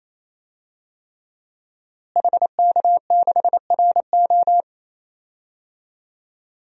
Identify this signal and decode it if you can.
Morse code, heard as a keyed tone, about 28 wpm, 700 Hz. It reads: HK6RO
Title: DXCC-100 Call Signs - 5 Characters - Repeated 3x - 28wpm